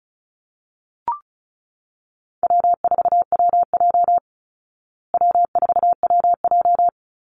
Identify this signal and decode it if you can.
Morse code, heard as a keyed tone, about 34 wpm, 700 Hz. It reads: W4WJ W4WJ
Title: Call Signs Repeated 2x - 35wpm